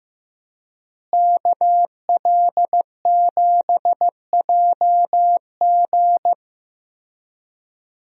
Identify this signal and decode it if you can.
Morse code, heard as a keyed tone, about 15 wpm, 700 Hz. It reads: KL7JG
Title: DXCC-100 Call Signs - 5 Characters - Repeated 2x - 15wpm